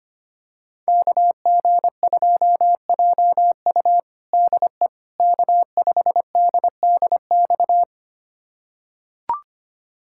Morse code, heard as a keyed tone, about 25 wpm, 700 Hz. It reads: KG2JU DE K5DDX
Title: USA Callsign DE Callsign - 5 Characters - 25wpm